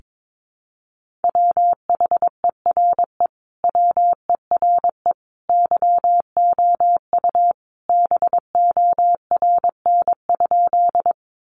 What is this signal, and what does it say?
Morse code, heard as a keyed tone, about 22 wpm, 700 Hz. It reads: WHERE WERE YOU BORN?
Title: Sentences from Top 1000 Words 22wpm